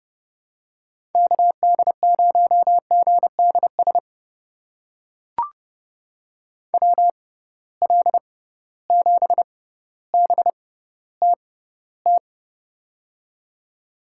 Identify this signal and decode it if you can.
Morse code, heard as a keyed tone, about 30 wpm, spaced as for 15 wpm, 700 Hz. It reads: KD0GDS WL7BTT
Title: USA Call Signs - 6 Characters - 6x Character Spacing - 30wpm